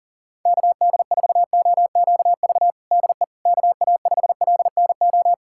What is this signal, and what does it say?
Morse code, heard as a keyed tone, about 40 wpm, 700 Hz. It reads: KD4OQV DE KA5LNO
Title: USA Callsign DE Callsign - 6 Characters - 40wpm